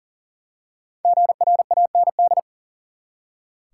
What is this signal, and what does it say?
Morse code, heard as a keyed tone, about 40 wpm, 700 Hz. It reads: GRAND
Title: Five Letter Words - 40wpm